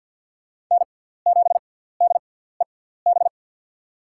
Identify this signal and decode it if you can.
Morse code, heard as a keyed tone, about 50 wpm, spaced as for 17 wpm, 700 Hz. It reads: N7DEB
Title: USA Call Signs - 5 Characters - 6x Character Spacing - 50wpm